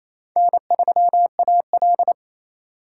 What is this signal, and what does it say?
Morse code, heard as a keyed tone, about 28 wpm, 700 Hz. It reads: N3AL
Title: USA Call Signs - 4 Characters - 2x Character Spacing - 28wpm